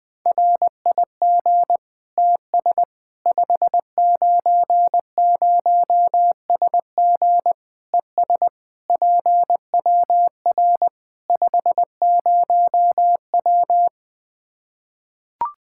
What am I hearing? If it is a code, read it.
Morse code, heard as a keyed tone, about 20 wpm, 700 Hz. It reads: RIG TS 590SG ES PWR 50W
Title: Practice QSO - Exchange 2 - Info - 20wpm